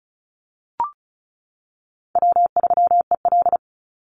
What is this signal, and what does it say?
Morse code, heard as a keyed tone, about 34 wpm, 700 Hz. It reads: W3EL
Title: Call Signs - 35wpm